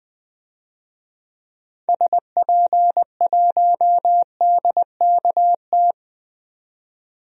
Morse code, heard as a keyed tone, about 20 wpm, 700 Hz. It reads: SP1DKT
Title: DXCC-100 Call Signs - 6 Characters - 20wpm